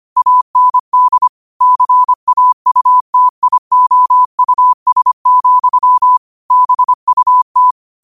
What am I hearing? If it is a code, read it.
Morse code, heard as a keyed tone, about 25 wpm, 1000 Hz. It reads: AND CAUTIOUS, BUT